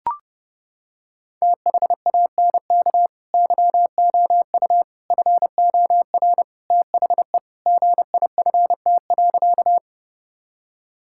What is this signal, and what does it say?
Morse code, heard as a keyed tone, about 30 wpm, 700 Hz. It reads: THANK YOU FOR THE GIFT.
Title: Sentences from Top 1000 Words - Review Part 6 - 30wpm